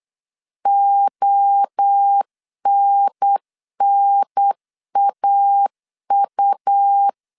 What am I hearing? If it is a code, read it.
Morse code, heard as a keyed tone, about 8 wpm, 800 Hz. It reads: ONNAU
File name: CW / DET / 1 3 1500C S A 8wpm 4229